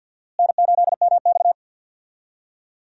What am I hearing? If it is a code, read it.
Morse code, heard as a keyed tone, about 52 wpm, 700 Hz. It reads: N8MX
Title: USA Call Signs - 4 Characters - Repeated 2x - 50wpm